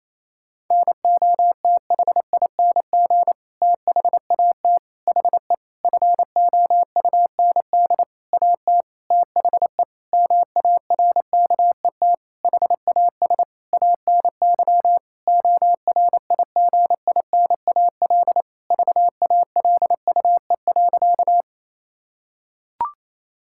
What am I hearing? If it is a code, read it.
Morse code, heard as a keyed tone, about 28 wpm, 700 Hz. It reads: NOTHING THAT HE FOUND AT THE MARKET HAS ANY ORIGINAL VALUE.
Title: Sentences from Top 1000 Words - Review Part 6 - 28wpm